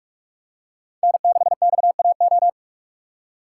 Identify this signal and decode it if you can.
Morse code, heard as a keyed tone, about 46 wpm, 700 Hz. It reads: N6XAO
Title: USA Call Signs - 3x Character Spacing - 45wpm